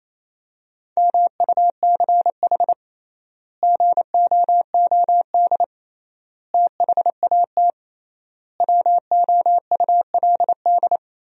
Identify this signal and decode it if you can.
Morse code, heard as a keyed tone, about 28 wpm, 700 Hz. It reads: MUCH GOOD THAT WOULD
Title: 3x Word-Spacing - Sentences from Top 200 Words - 28wpm